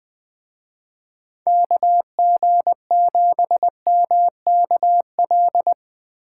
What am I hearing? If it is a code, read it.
Morse code, heard as a keyed tone, about 22 wpm, 700 Hz. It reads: KG7MKL
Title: Speed Racing - USA Call Signs 20wpm